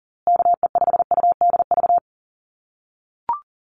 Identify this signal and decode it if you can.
Morse code, heard as a keyed tone, about 40 wpm, 700 Hz. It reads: KE5UDV
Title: USA Call Signs Repeated 2x 40wpm